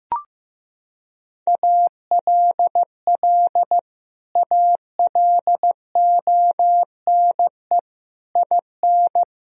Morse code, heard as a keyed tone, about 15 wpm, 700 Hz. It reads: ALL ALONE IN
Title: Sentences from Top 1000 Words - Review Part 2 - 15wpm